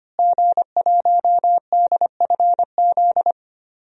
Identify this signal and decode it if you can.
Morse code, heard as a keyed tone, about 25 wpm, 700 Hz. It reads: G1DFZ
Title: DXCC-100 Call Signs - 5 Characters - Repeated 3x - 25wpm